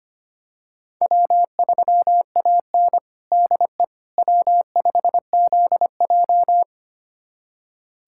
Morse code, heard as a keyed tone, about 25 wpm, 700 Hz. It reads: W3AN DE W5ZJ
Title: USA Callsign DE Callsign - 4 Characters - 25wpm